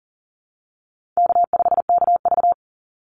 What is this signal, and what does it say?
Morse code, heard as a keyed tone, about 40 wpm, 700 Hz. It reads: K5KV
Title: Easy Call Signs Repeated 2x - 40wpm